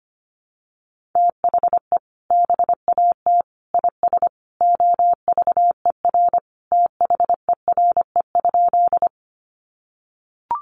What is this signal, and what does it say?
Morse code, heard as a keyed tone, about 25 wpm, 700 Hz. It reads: THE BAT IS OVER THERE?
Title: Sentences from Top 1500 Words 25wpm